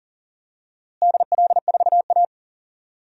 Morse code, heard as a keyed tone, about 40 wpm, 700 Hz. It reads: DL4A
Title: DXCC-100 Call Signs - 4 Characters - Repeated 2x - 40wpm